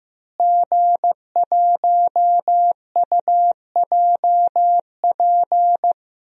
Morse code, heard as a keyed tone, about 15 wpm, 700 Hz. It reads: G1UJP
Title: DXCC-100 Call Signs - 5 Characters - Repeated 3x - 15wpm